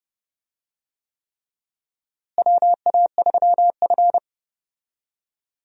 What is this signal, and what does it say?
Morse code, heard as a keyed tone, about 32 wpm, 700 Hz. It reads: WA3F
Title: USA Call Signs - 4 Characters - Repeated 3x - 30wpm